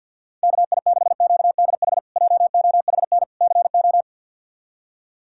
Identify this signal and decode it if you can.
Morse code, heard as a keyed tone, about 52 wpm, 700 Hz. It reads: KI6QDH JOHN CO
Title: K1USN SST QSO Practice - 50wpm